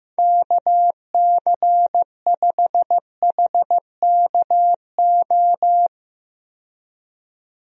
Morse code, heard as a keyed tone, about 15 wpm, 700 Hz. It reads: KC5HKO
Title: USA Call Signs - 6 Characters - 15wpm